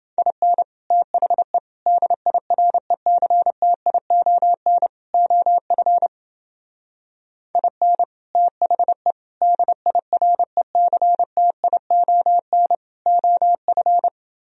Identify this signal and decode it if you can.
Morse code, heard as a keyed tone, about 28 wpm, 700 Hz. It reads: IN THE DIRECTION OF IN THE DIRECTION OF